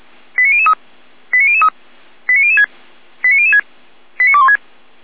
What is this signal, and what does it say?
Five-tone selective call (ZVEI), heard as quick runs of five tones, tones of about 70 ms. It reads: 89003 89003 89006 89006 89216